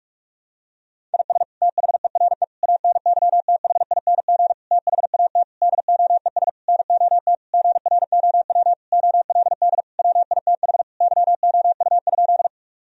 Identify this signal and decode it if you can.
Morse code, heard as a keyed tone, about 41 wpm, 700 Hz. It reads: IS THERE ANYTHING THAT DOES NOT GROW OLD WITH YOU?
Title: Sentences from Top 1000 Words - Review Part 1 - 45wpm